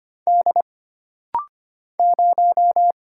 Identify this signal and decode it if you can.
Morse code, heard as a keyed tone, about 25 wpm, 700 Hz. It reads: D 0
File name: Single Character (Letter:Number:Punctuation:Prosign) - Rapid-Fire - 25wpm